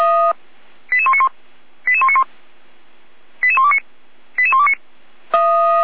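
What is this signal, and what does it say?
Five-tone selective call (ZVEI), heard as quick runs of five tones, tones of about 70 ms. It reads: 88191 88191 88129 88129
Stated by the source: ZVEI five-tone call received on SDR.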